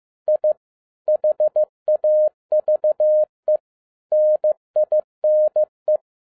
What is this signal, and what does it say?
Morse code, heard as a keyed tone, about 15 wpm, 610 Hz. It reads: I HAVE NINE
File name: 2011-07-mp3 / n22 QRP OTAKU 15wpm